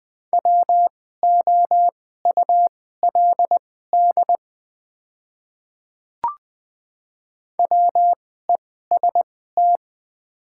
Morse code, heard as a keyed tone, about 20 wpm, spaced as for 14 wpm, 700 Hz. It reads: WOULD WEST
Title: 2x Character Spacing - Top 500 Words - 20wpm